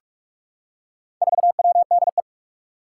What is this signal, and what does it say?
Morse code, heard as a keyed tone, about 44 wpm, 700 Hz. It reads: 4WDE
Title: Four Challenging Letter-Number Combinations 45wpm